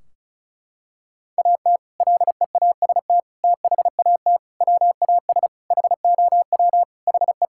Morse code, heard as a keyed tone, about 34 wpm, 700 Hz. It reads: AT LEAST THAT WAS HOW HE
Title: Sentences from Top 1600 Words 35wpm